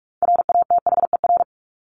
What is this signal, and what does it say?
Morse code, heard as a keyed tone, about 44 wpm, 700 Hz. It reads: RATHER
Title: Top 501-750 Words 45wpm